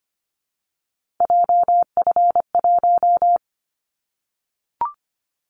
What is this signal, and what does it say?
Morse code, heard as a keyed tone, about 25 wpm, 700 Hz. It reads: JF1
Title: Callsign Prefixes for Top 25 DXCC - Repeated 3x - 25wpm